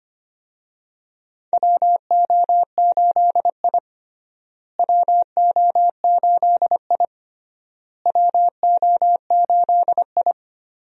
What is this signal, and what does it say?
Morse code, heard as a keyed tone, about 25 wpm, 700 Hz. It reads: WO8I WO8I WO8I
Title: USA Call Signs - 4 Characters - Repeated 3x - 25wpm